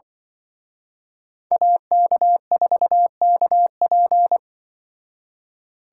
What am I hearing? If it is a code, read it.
Morse code, heard as a keyed tone, about 25 wpm, 700 Hz. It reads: AK4KP
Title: Speed Racing - USA 2x2 Call Signs 20wpm